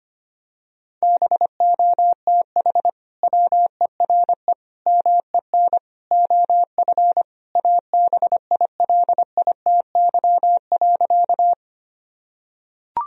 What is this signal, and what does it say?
Morse code, heard as a keyed tone, about 25 wpm, 700 Hz. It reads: BOTH WERE MEN OF ABILITY.
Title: Sentences from Top 1000 Words - Review Part 3 - 25wpm